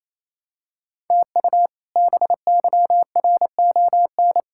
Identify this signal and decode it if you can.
Morse code, heard as a keyed tone, about 28 wpm, 700 Hz. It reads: TU BYRON
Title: K1USN SST QSO Practice - 28wpm